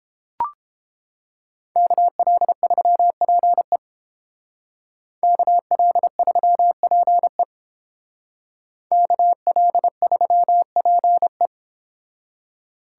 Morse code, heard as a keyed tone, about 31 wpm, 700 Hz. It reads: KL3PE KL3PE KL3PE